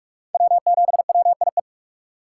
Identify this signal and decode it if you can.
Morse code, heard as a keyed tone, about 46 wpm, 700 Hz. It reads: W7WIE
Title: USA Call Signs - 5 Characters - 4x Character Spacing - 45wpm